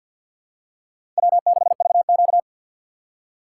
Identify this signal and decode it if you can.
Morse code, heard as a keyed tone, about 48 wpm, 700 Hz. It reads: W6VQ
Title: USA Call Signs - 4x Character Spacing - 50wpm